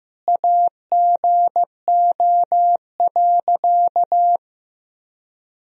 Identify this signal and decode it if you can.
Morse code, heard as a keyed tone, about 15 wpm, 700 Hz. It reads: AGO.
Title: Sentences from Top 1000 Words - Review Part 3 - 15wpm